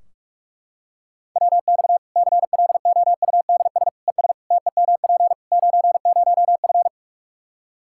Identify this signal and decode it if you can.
Morse code, heard as a keyed tone, about 41 wpm, 700 Hz. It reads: WX CLOUDS ES TEMP 90F
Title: Practice QSO - Exchange 2 - Info - 45wpm